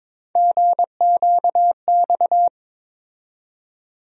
Morse code, heard as a keyed tone, about 22 wpm, 700 Hz. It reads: GQX
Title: USA Call Sign Postfixes - 3 Letters 22wpm